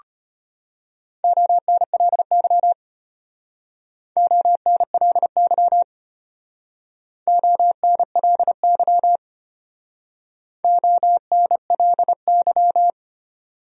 Morse code, heard as a keyed tone, about 32 wpm, 700 Hz. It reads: ONLY ONLY ONLY ONLY